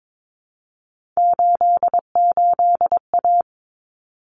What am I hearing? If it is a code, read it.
Morse code, heard as a keyed tone, about 22 wpm, 700 Hz. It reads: 88A
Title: Common Cut Numbers - 3-Digit 22wpm